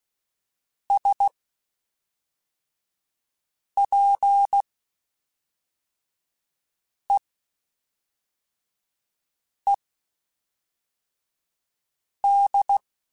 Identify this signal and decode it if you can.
Morse code, heard as a keyed tone, about 16 wpm, spaced as for 3 wpm, 800 Hz. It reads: SPEED
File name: Spaced Characters / Spaced Characters-mp3s / Weather5 5 sp @5